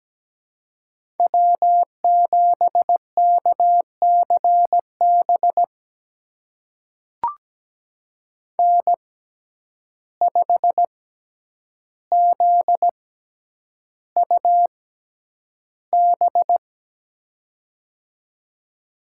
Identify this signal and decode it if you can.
Morse code, heard as a keyed tone, about 17 wpm, spaced as for 9 wpm, 700 Hz. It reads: W7KCB N5ZUB
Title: USA Call Signs - 5 Characters - 6x Character Spacing - 17wpm